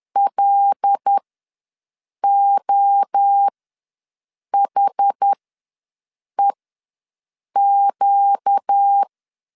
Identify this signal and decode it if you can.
Morse code, heard as a keyed tone, about 10 wpm, spaced as for 6 wpm, 800 Hz. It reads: LOHEQ
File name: LEZIONI / 5 LBVC / 5 L ws 10wpm 3636